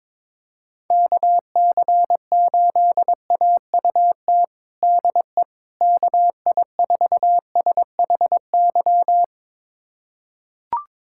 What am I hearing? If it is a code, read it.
Morse code, heard as a keyed tone, about 22 wpm, 700 Hz. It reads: KC8AUT DE KI4SHY
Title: USA Callsign DE Callsign - 6 Characters - 22wpm